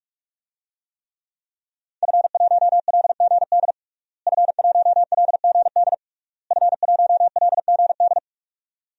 Morse code, heard as a keyed tone, about 47 wpm, 700 Hz. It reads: F1LGD F1LGD F1LGD
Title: DXCC-100 Call Signs - 5 Characters - Repeated 3x - 45wpm